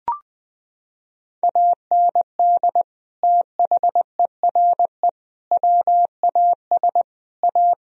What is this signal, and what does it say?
Morse code, heard as a keyed tone, about 21 wpm, 700 Hz. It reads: AND THERE WAS A
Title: Sentences from Top 1000 Words - Review Part 3 - 20wpm